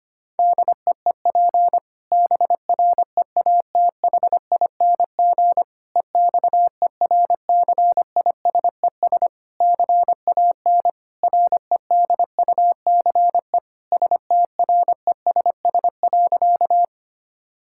Morse code, heard as a keyed tone, about 25 wpm, 700 Hz. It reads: DEEP BREATHING EXERCISES CAN REDUCE STRESS.
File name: practice / 25wpm / Difficult Sentences with Hints - 25wpm